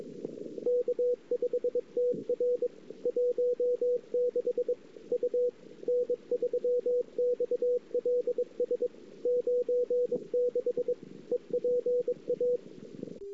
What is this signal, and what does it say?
Morse code, heard as a keyed tone, about 22 wpm, 500 Hz. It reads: K5TR 16 U N3XLS 96 EPA
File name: CW / 2008-ss-cw-k5tr-op-n5rz -14:00